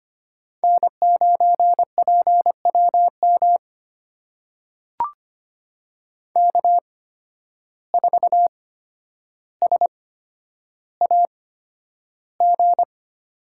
Morse code, heard as a keyed tone, about 25 wpm, spaced as for 11 wpm, 700 Hz. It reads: N9PWM K4SAG
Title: USA Call Signs - 5 Characters - 8x Character Spacing - 25wpm